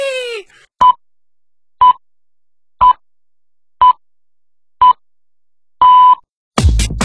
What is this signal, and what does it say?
Radio time signal (time pips). It is heard from 0.8 to 6.1 s.